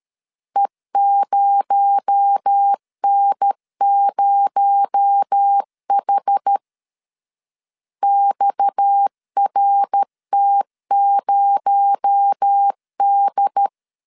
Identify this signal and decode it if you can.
Morse code, heard as a keyed tone, about 13 wpm, 800 Hz. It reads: E0N0H XRT0D
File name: LEZIONI / 6 PKX0 / 6 PKX0 12wpm 2231